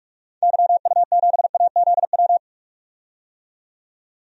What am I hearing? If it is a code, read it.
Morse code, heard as a keyed tone, about 42 wpm, 700 Hz. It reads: YU7AZW
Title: DXCC-100 Call Signs - 6 Characters - 45wpm